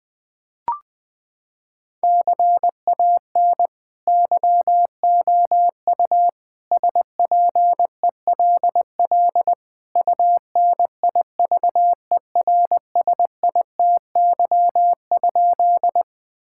Morse code, heard as a keyed tone, about 20 wpm, 700 Hz. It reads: CAN YOU SPELL UNIVERSITY?
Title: Sentences from Top 1000 Words - Review Part 5 - 20wpm